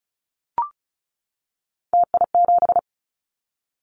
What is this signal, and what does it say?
Morse code, heard as a keyed tone, about 34 wpm, 700 Hz. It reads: TI7
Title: Callsign Prefixes for Top 100 DXCC - 35wpm